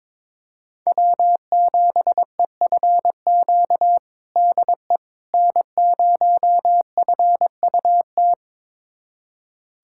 Morse code, heard as a keyed tone, about 22 wpm, 700 Hz. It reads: W7EFQ DE N0FUT
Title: USA Callsign DE Callsign - 5 Characters - 22wpm